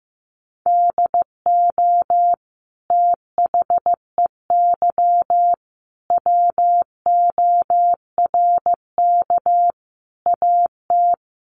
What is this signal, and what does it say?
Morse code, heard as a keyed tone, about 15 wpm, 700 Hz. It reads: DO THEY WORK AT
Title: Sentences from Top 1300 Words 15wpm